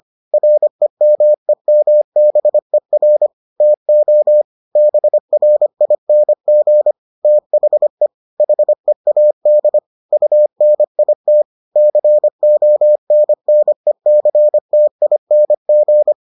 Morse code, heard as a keyed tone, about 25 wpm, 600 Hz. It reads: REMEMBER TO BRING THE HEAD UNIT CONNECTING